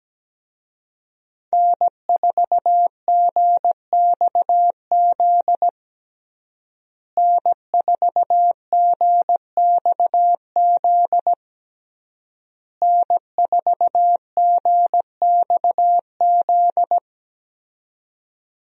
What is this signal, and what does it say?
Morse code, heard as a keyed tone, about 17 wpm, 700 Hz. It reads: N4GXZ N4GXZ N4GXZ
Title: USA Call Signs - 5 Characters - Repeated 3x - 17wpm